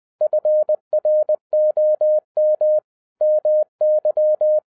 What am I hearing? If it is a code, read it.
Morse code, heard as a keyed tone, about 20 wpm, 610 Hz. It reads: FROM MY
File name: 2012-01-mp3 / n05-CAME BACK TO HAM RADIO AFTER 30 YEARS OF INTERVAL-20wpm